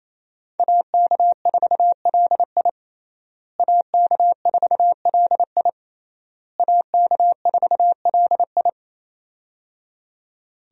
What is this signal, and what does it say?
Morse code, heard as a keyed tone, about 28 wpm, 700 Hz. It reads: AK4LI AK4LI AK4LI
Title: USA Call Signs - 5 Characters - Repeated 3x - 28wpm